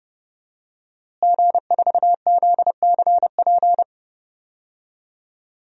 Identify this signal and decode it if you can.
Morse code, heard as a keyed tone, about 30 wpm, 700 Hz. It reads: G4ZCP
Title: DXCC-100 Call Signs - 5 Characters - Repeated 2x - 30wpm